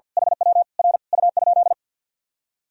Morse code, heard as a keyed tone, about 46 wpm, 700 Hz. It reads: HW R U?
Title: Sentences from Top 200 Words and Abbreviations 50wpm